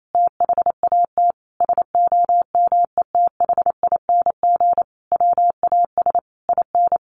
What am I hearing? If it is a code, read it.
Morse code, heard as a keyed tone, about 28 wpm, 700 Hz. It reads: THAT SOMETHING WAS IN
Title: Sentences from Top 1400 Words 28wpm